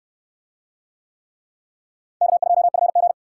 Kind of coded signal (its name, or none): Morse code